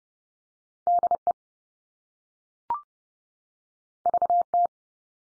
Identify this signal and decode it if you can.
Morse code, heard as a keyed tone, about 30 wpm, 700 Hz. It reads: DE VT
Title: US States 30WPM